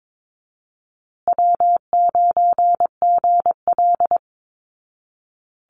Morse code, heard as a keyed tone, about 22 wpm, 700 Hz. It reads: W9GL
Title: Call Signs Repeated 2x - 22wpm